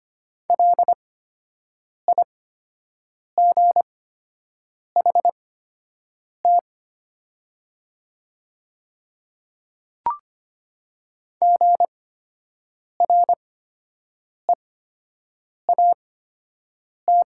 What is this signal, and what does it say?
Morse code, heard as a keyed tone, about 25 wpm, spaced as for 7 wpm, 700 Hz. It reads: LIGHT GREAT
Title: 8x Character Spacing - Top 300 Words - 25wpm